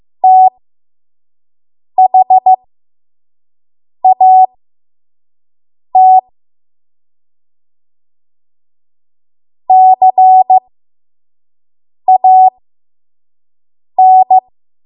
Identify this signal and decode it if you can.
Morse code, heard as a keyed tone, about 15 wpm, spaced as for 5 wpm, 750 Hz. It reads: THAT CAN
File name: Morse / 05WPM